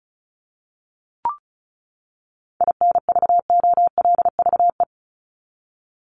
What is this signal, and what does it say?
Morse code, heard as a keyed tone, about 34 wpm, 700 Hz. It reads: INVOLVE
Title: Top 1001-1250 Words 35wpm